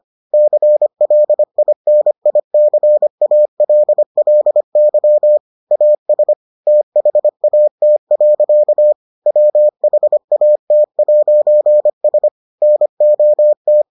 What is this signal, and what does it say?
Morse code, heard as a keyed tone, about 25 wpm, 600 Hz. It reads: CLINICALLY AS THAT. WHAT'S NOT